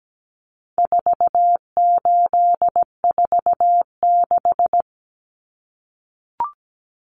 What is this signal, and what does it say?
Morse code, heard as a keyed tone, about 17 wpm, 700 Hz. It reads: 4846
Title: Common Cut Numbers - 4-Digit 17wpm